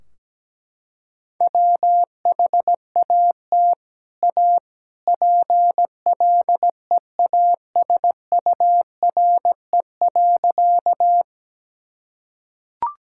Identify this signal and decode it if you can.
Morse code, heard as a keyed tone, about 17 wpm, 700 Hz. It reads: WHAT A PLEASURE.
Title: Sentences from Top 1600 Words 17wpm